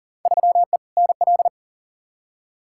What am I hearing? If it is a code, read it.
Morse code, heard as a keyed tone, about 40 wpm, 700 Hz. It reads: 3E NL
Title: Field Day Exchanges Encore - 40wpm